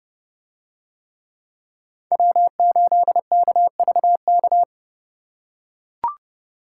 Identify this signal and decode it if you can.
Morse code, heard as a keyed tone, about 29 wpm, 700 Hz. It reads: W8KVK